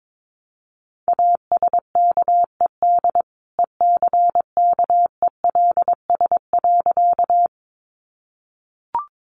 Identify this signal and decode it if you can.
Morse code, heard as a keyed tone, about 22 wpm, 700 Hz. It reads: ASKED ECKELS.
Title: A Sound Of Thunder 22wpm